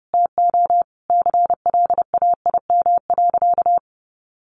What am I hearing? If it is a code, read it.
Morse code, heard as a keyed tone, about 30 wpm, 700 Hz. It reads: TO CLAIM.
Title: Sentences from Top 1500 Words 30wpm